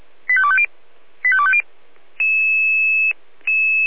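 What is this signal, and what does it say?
Five-tone selective call (ZVEI), heard as quick runs of five tones, tones of about 70 ms. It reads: 85250 85250
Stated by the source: ZVEI five-tone selective call 85259, received by SDR.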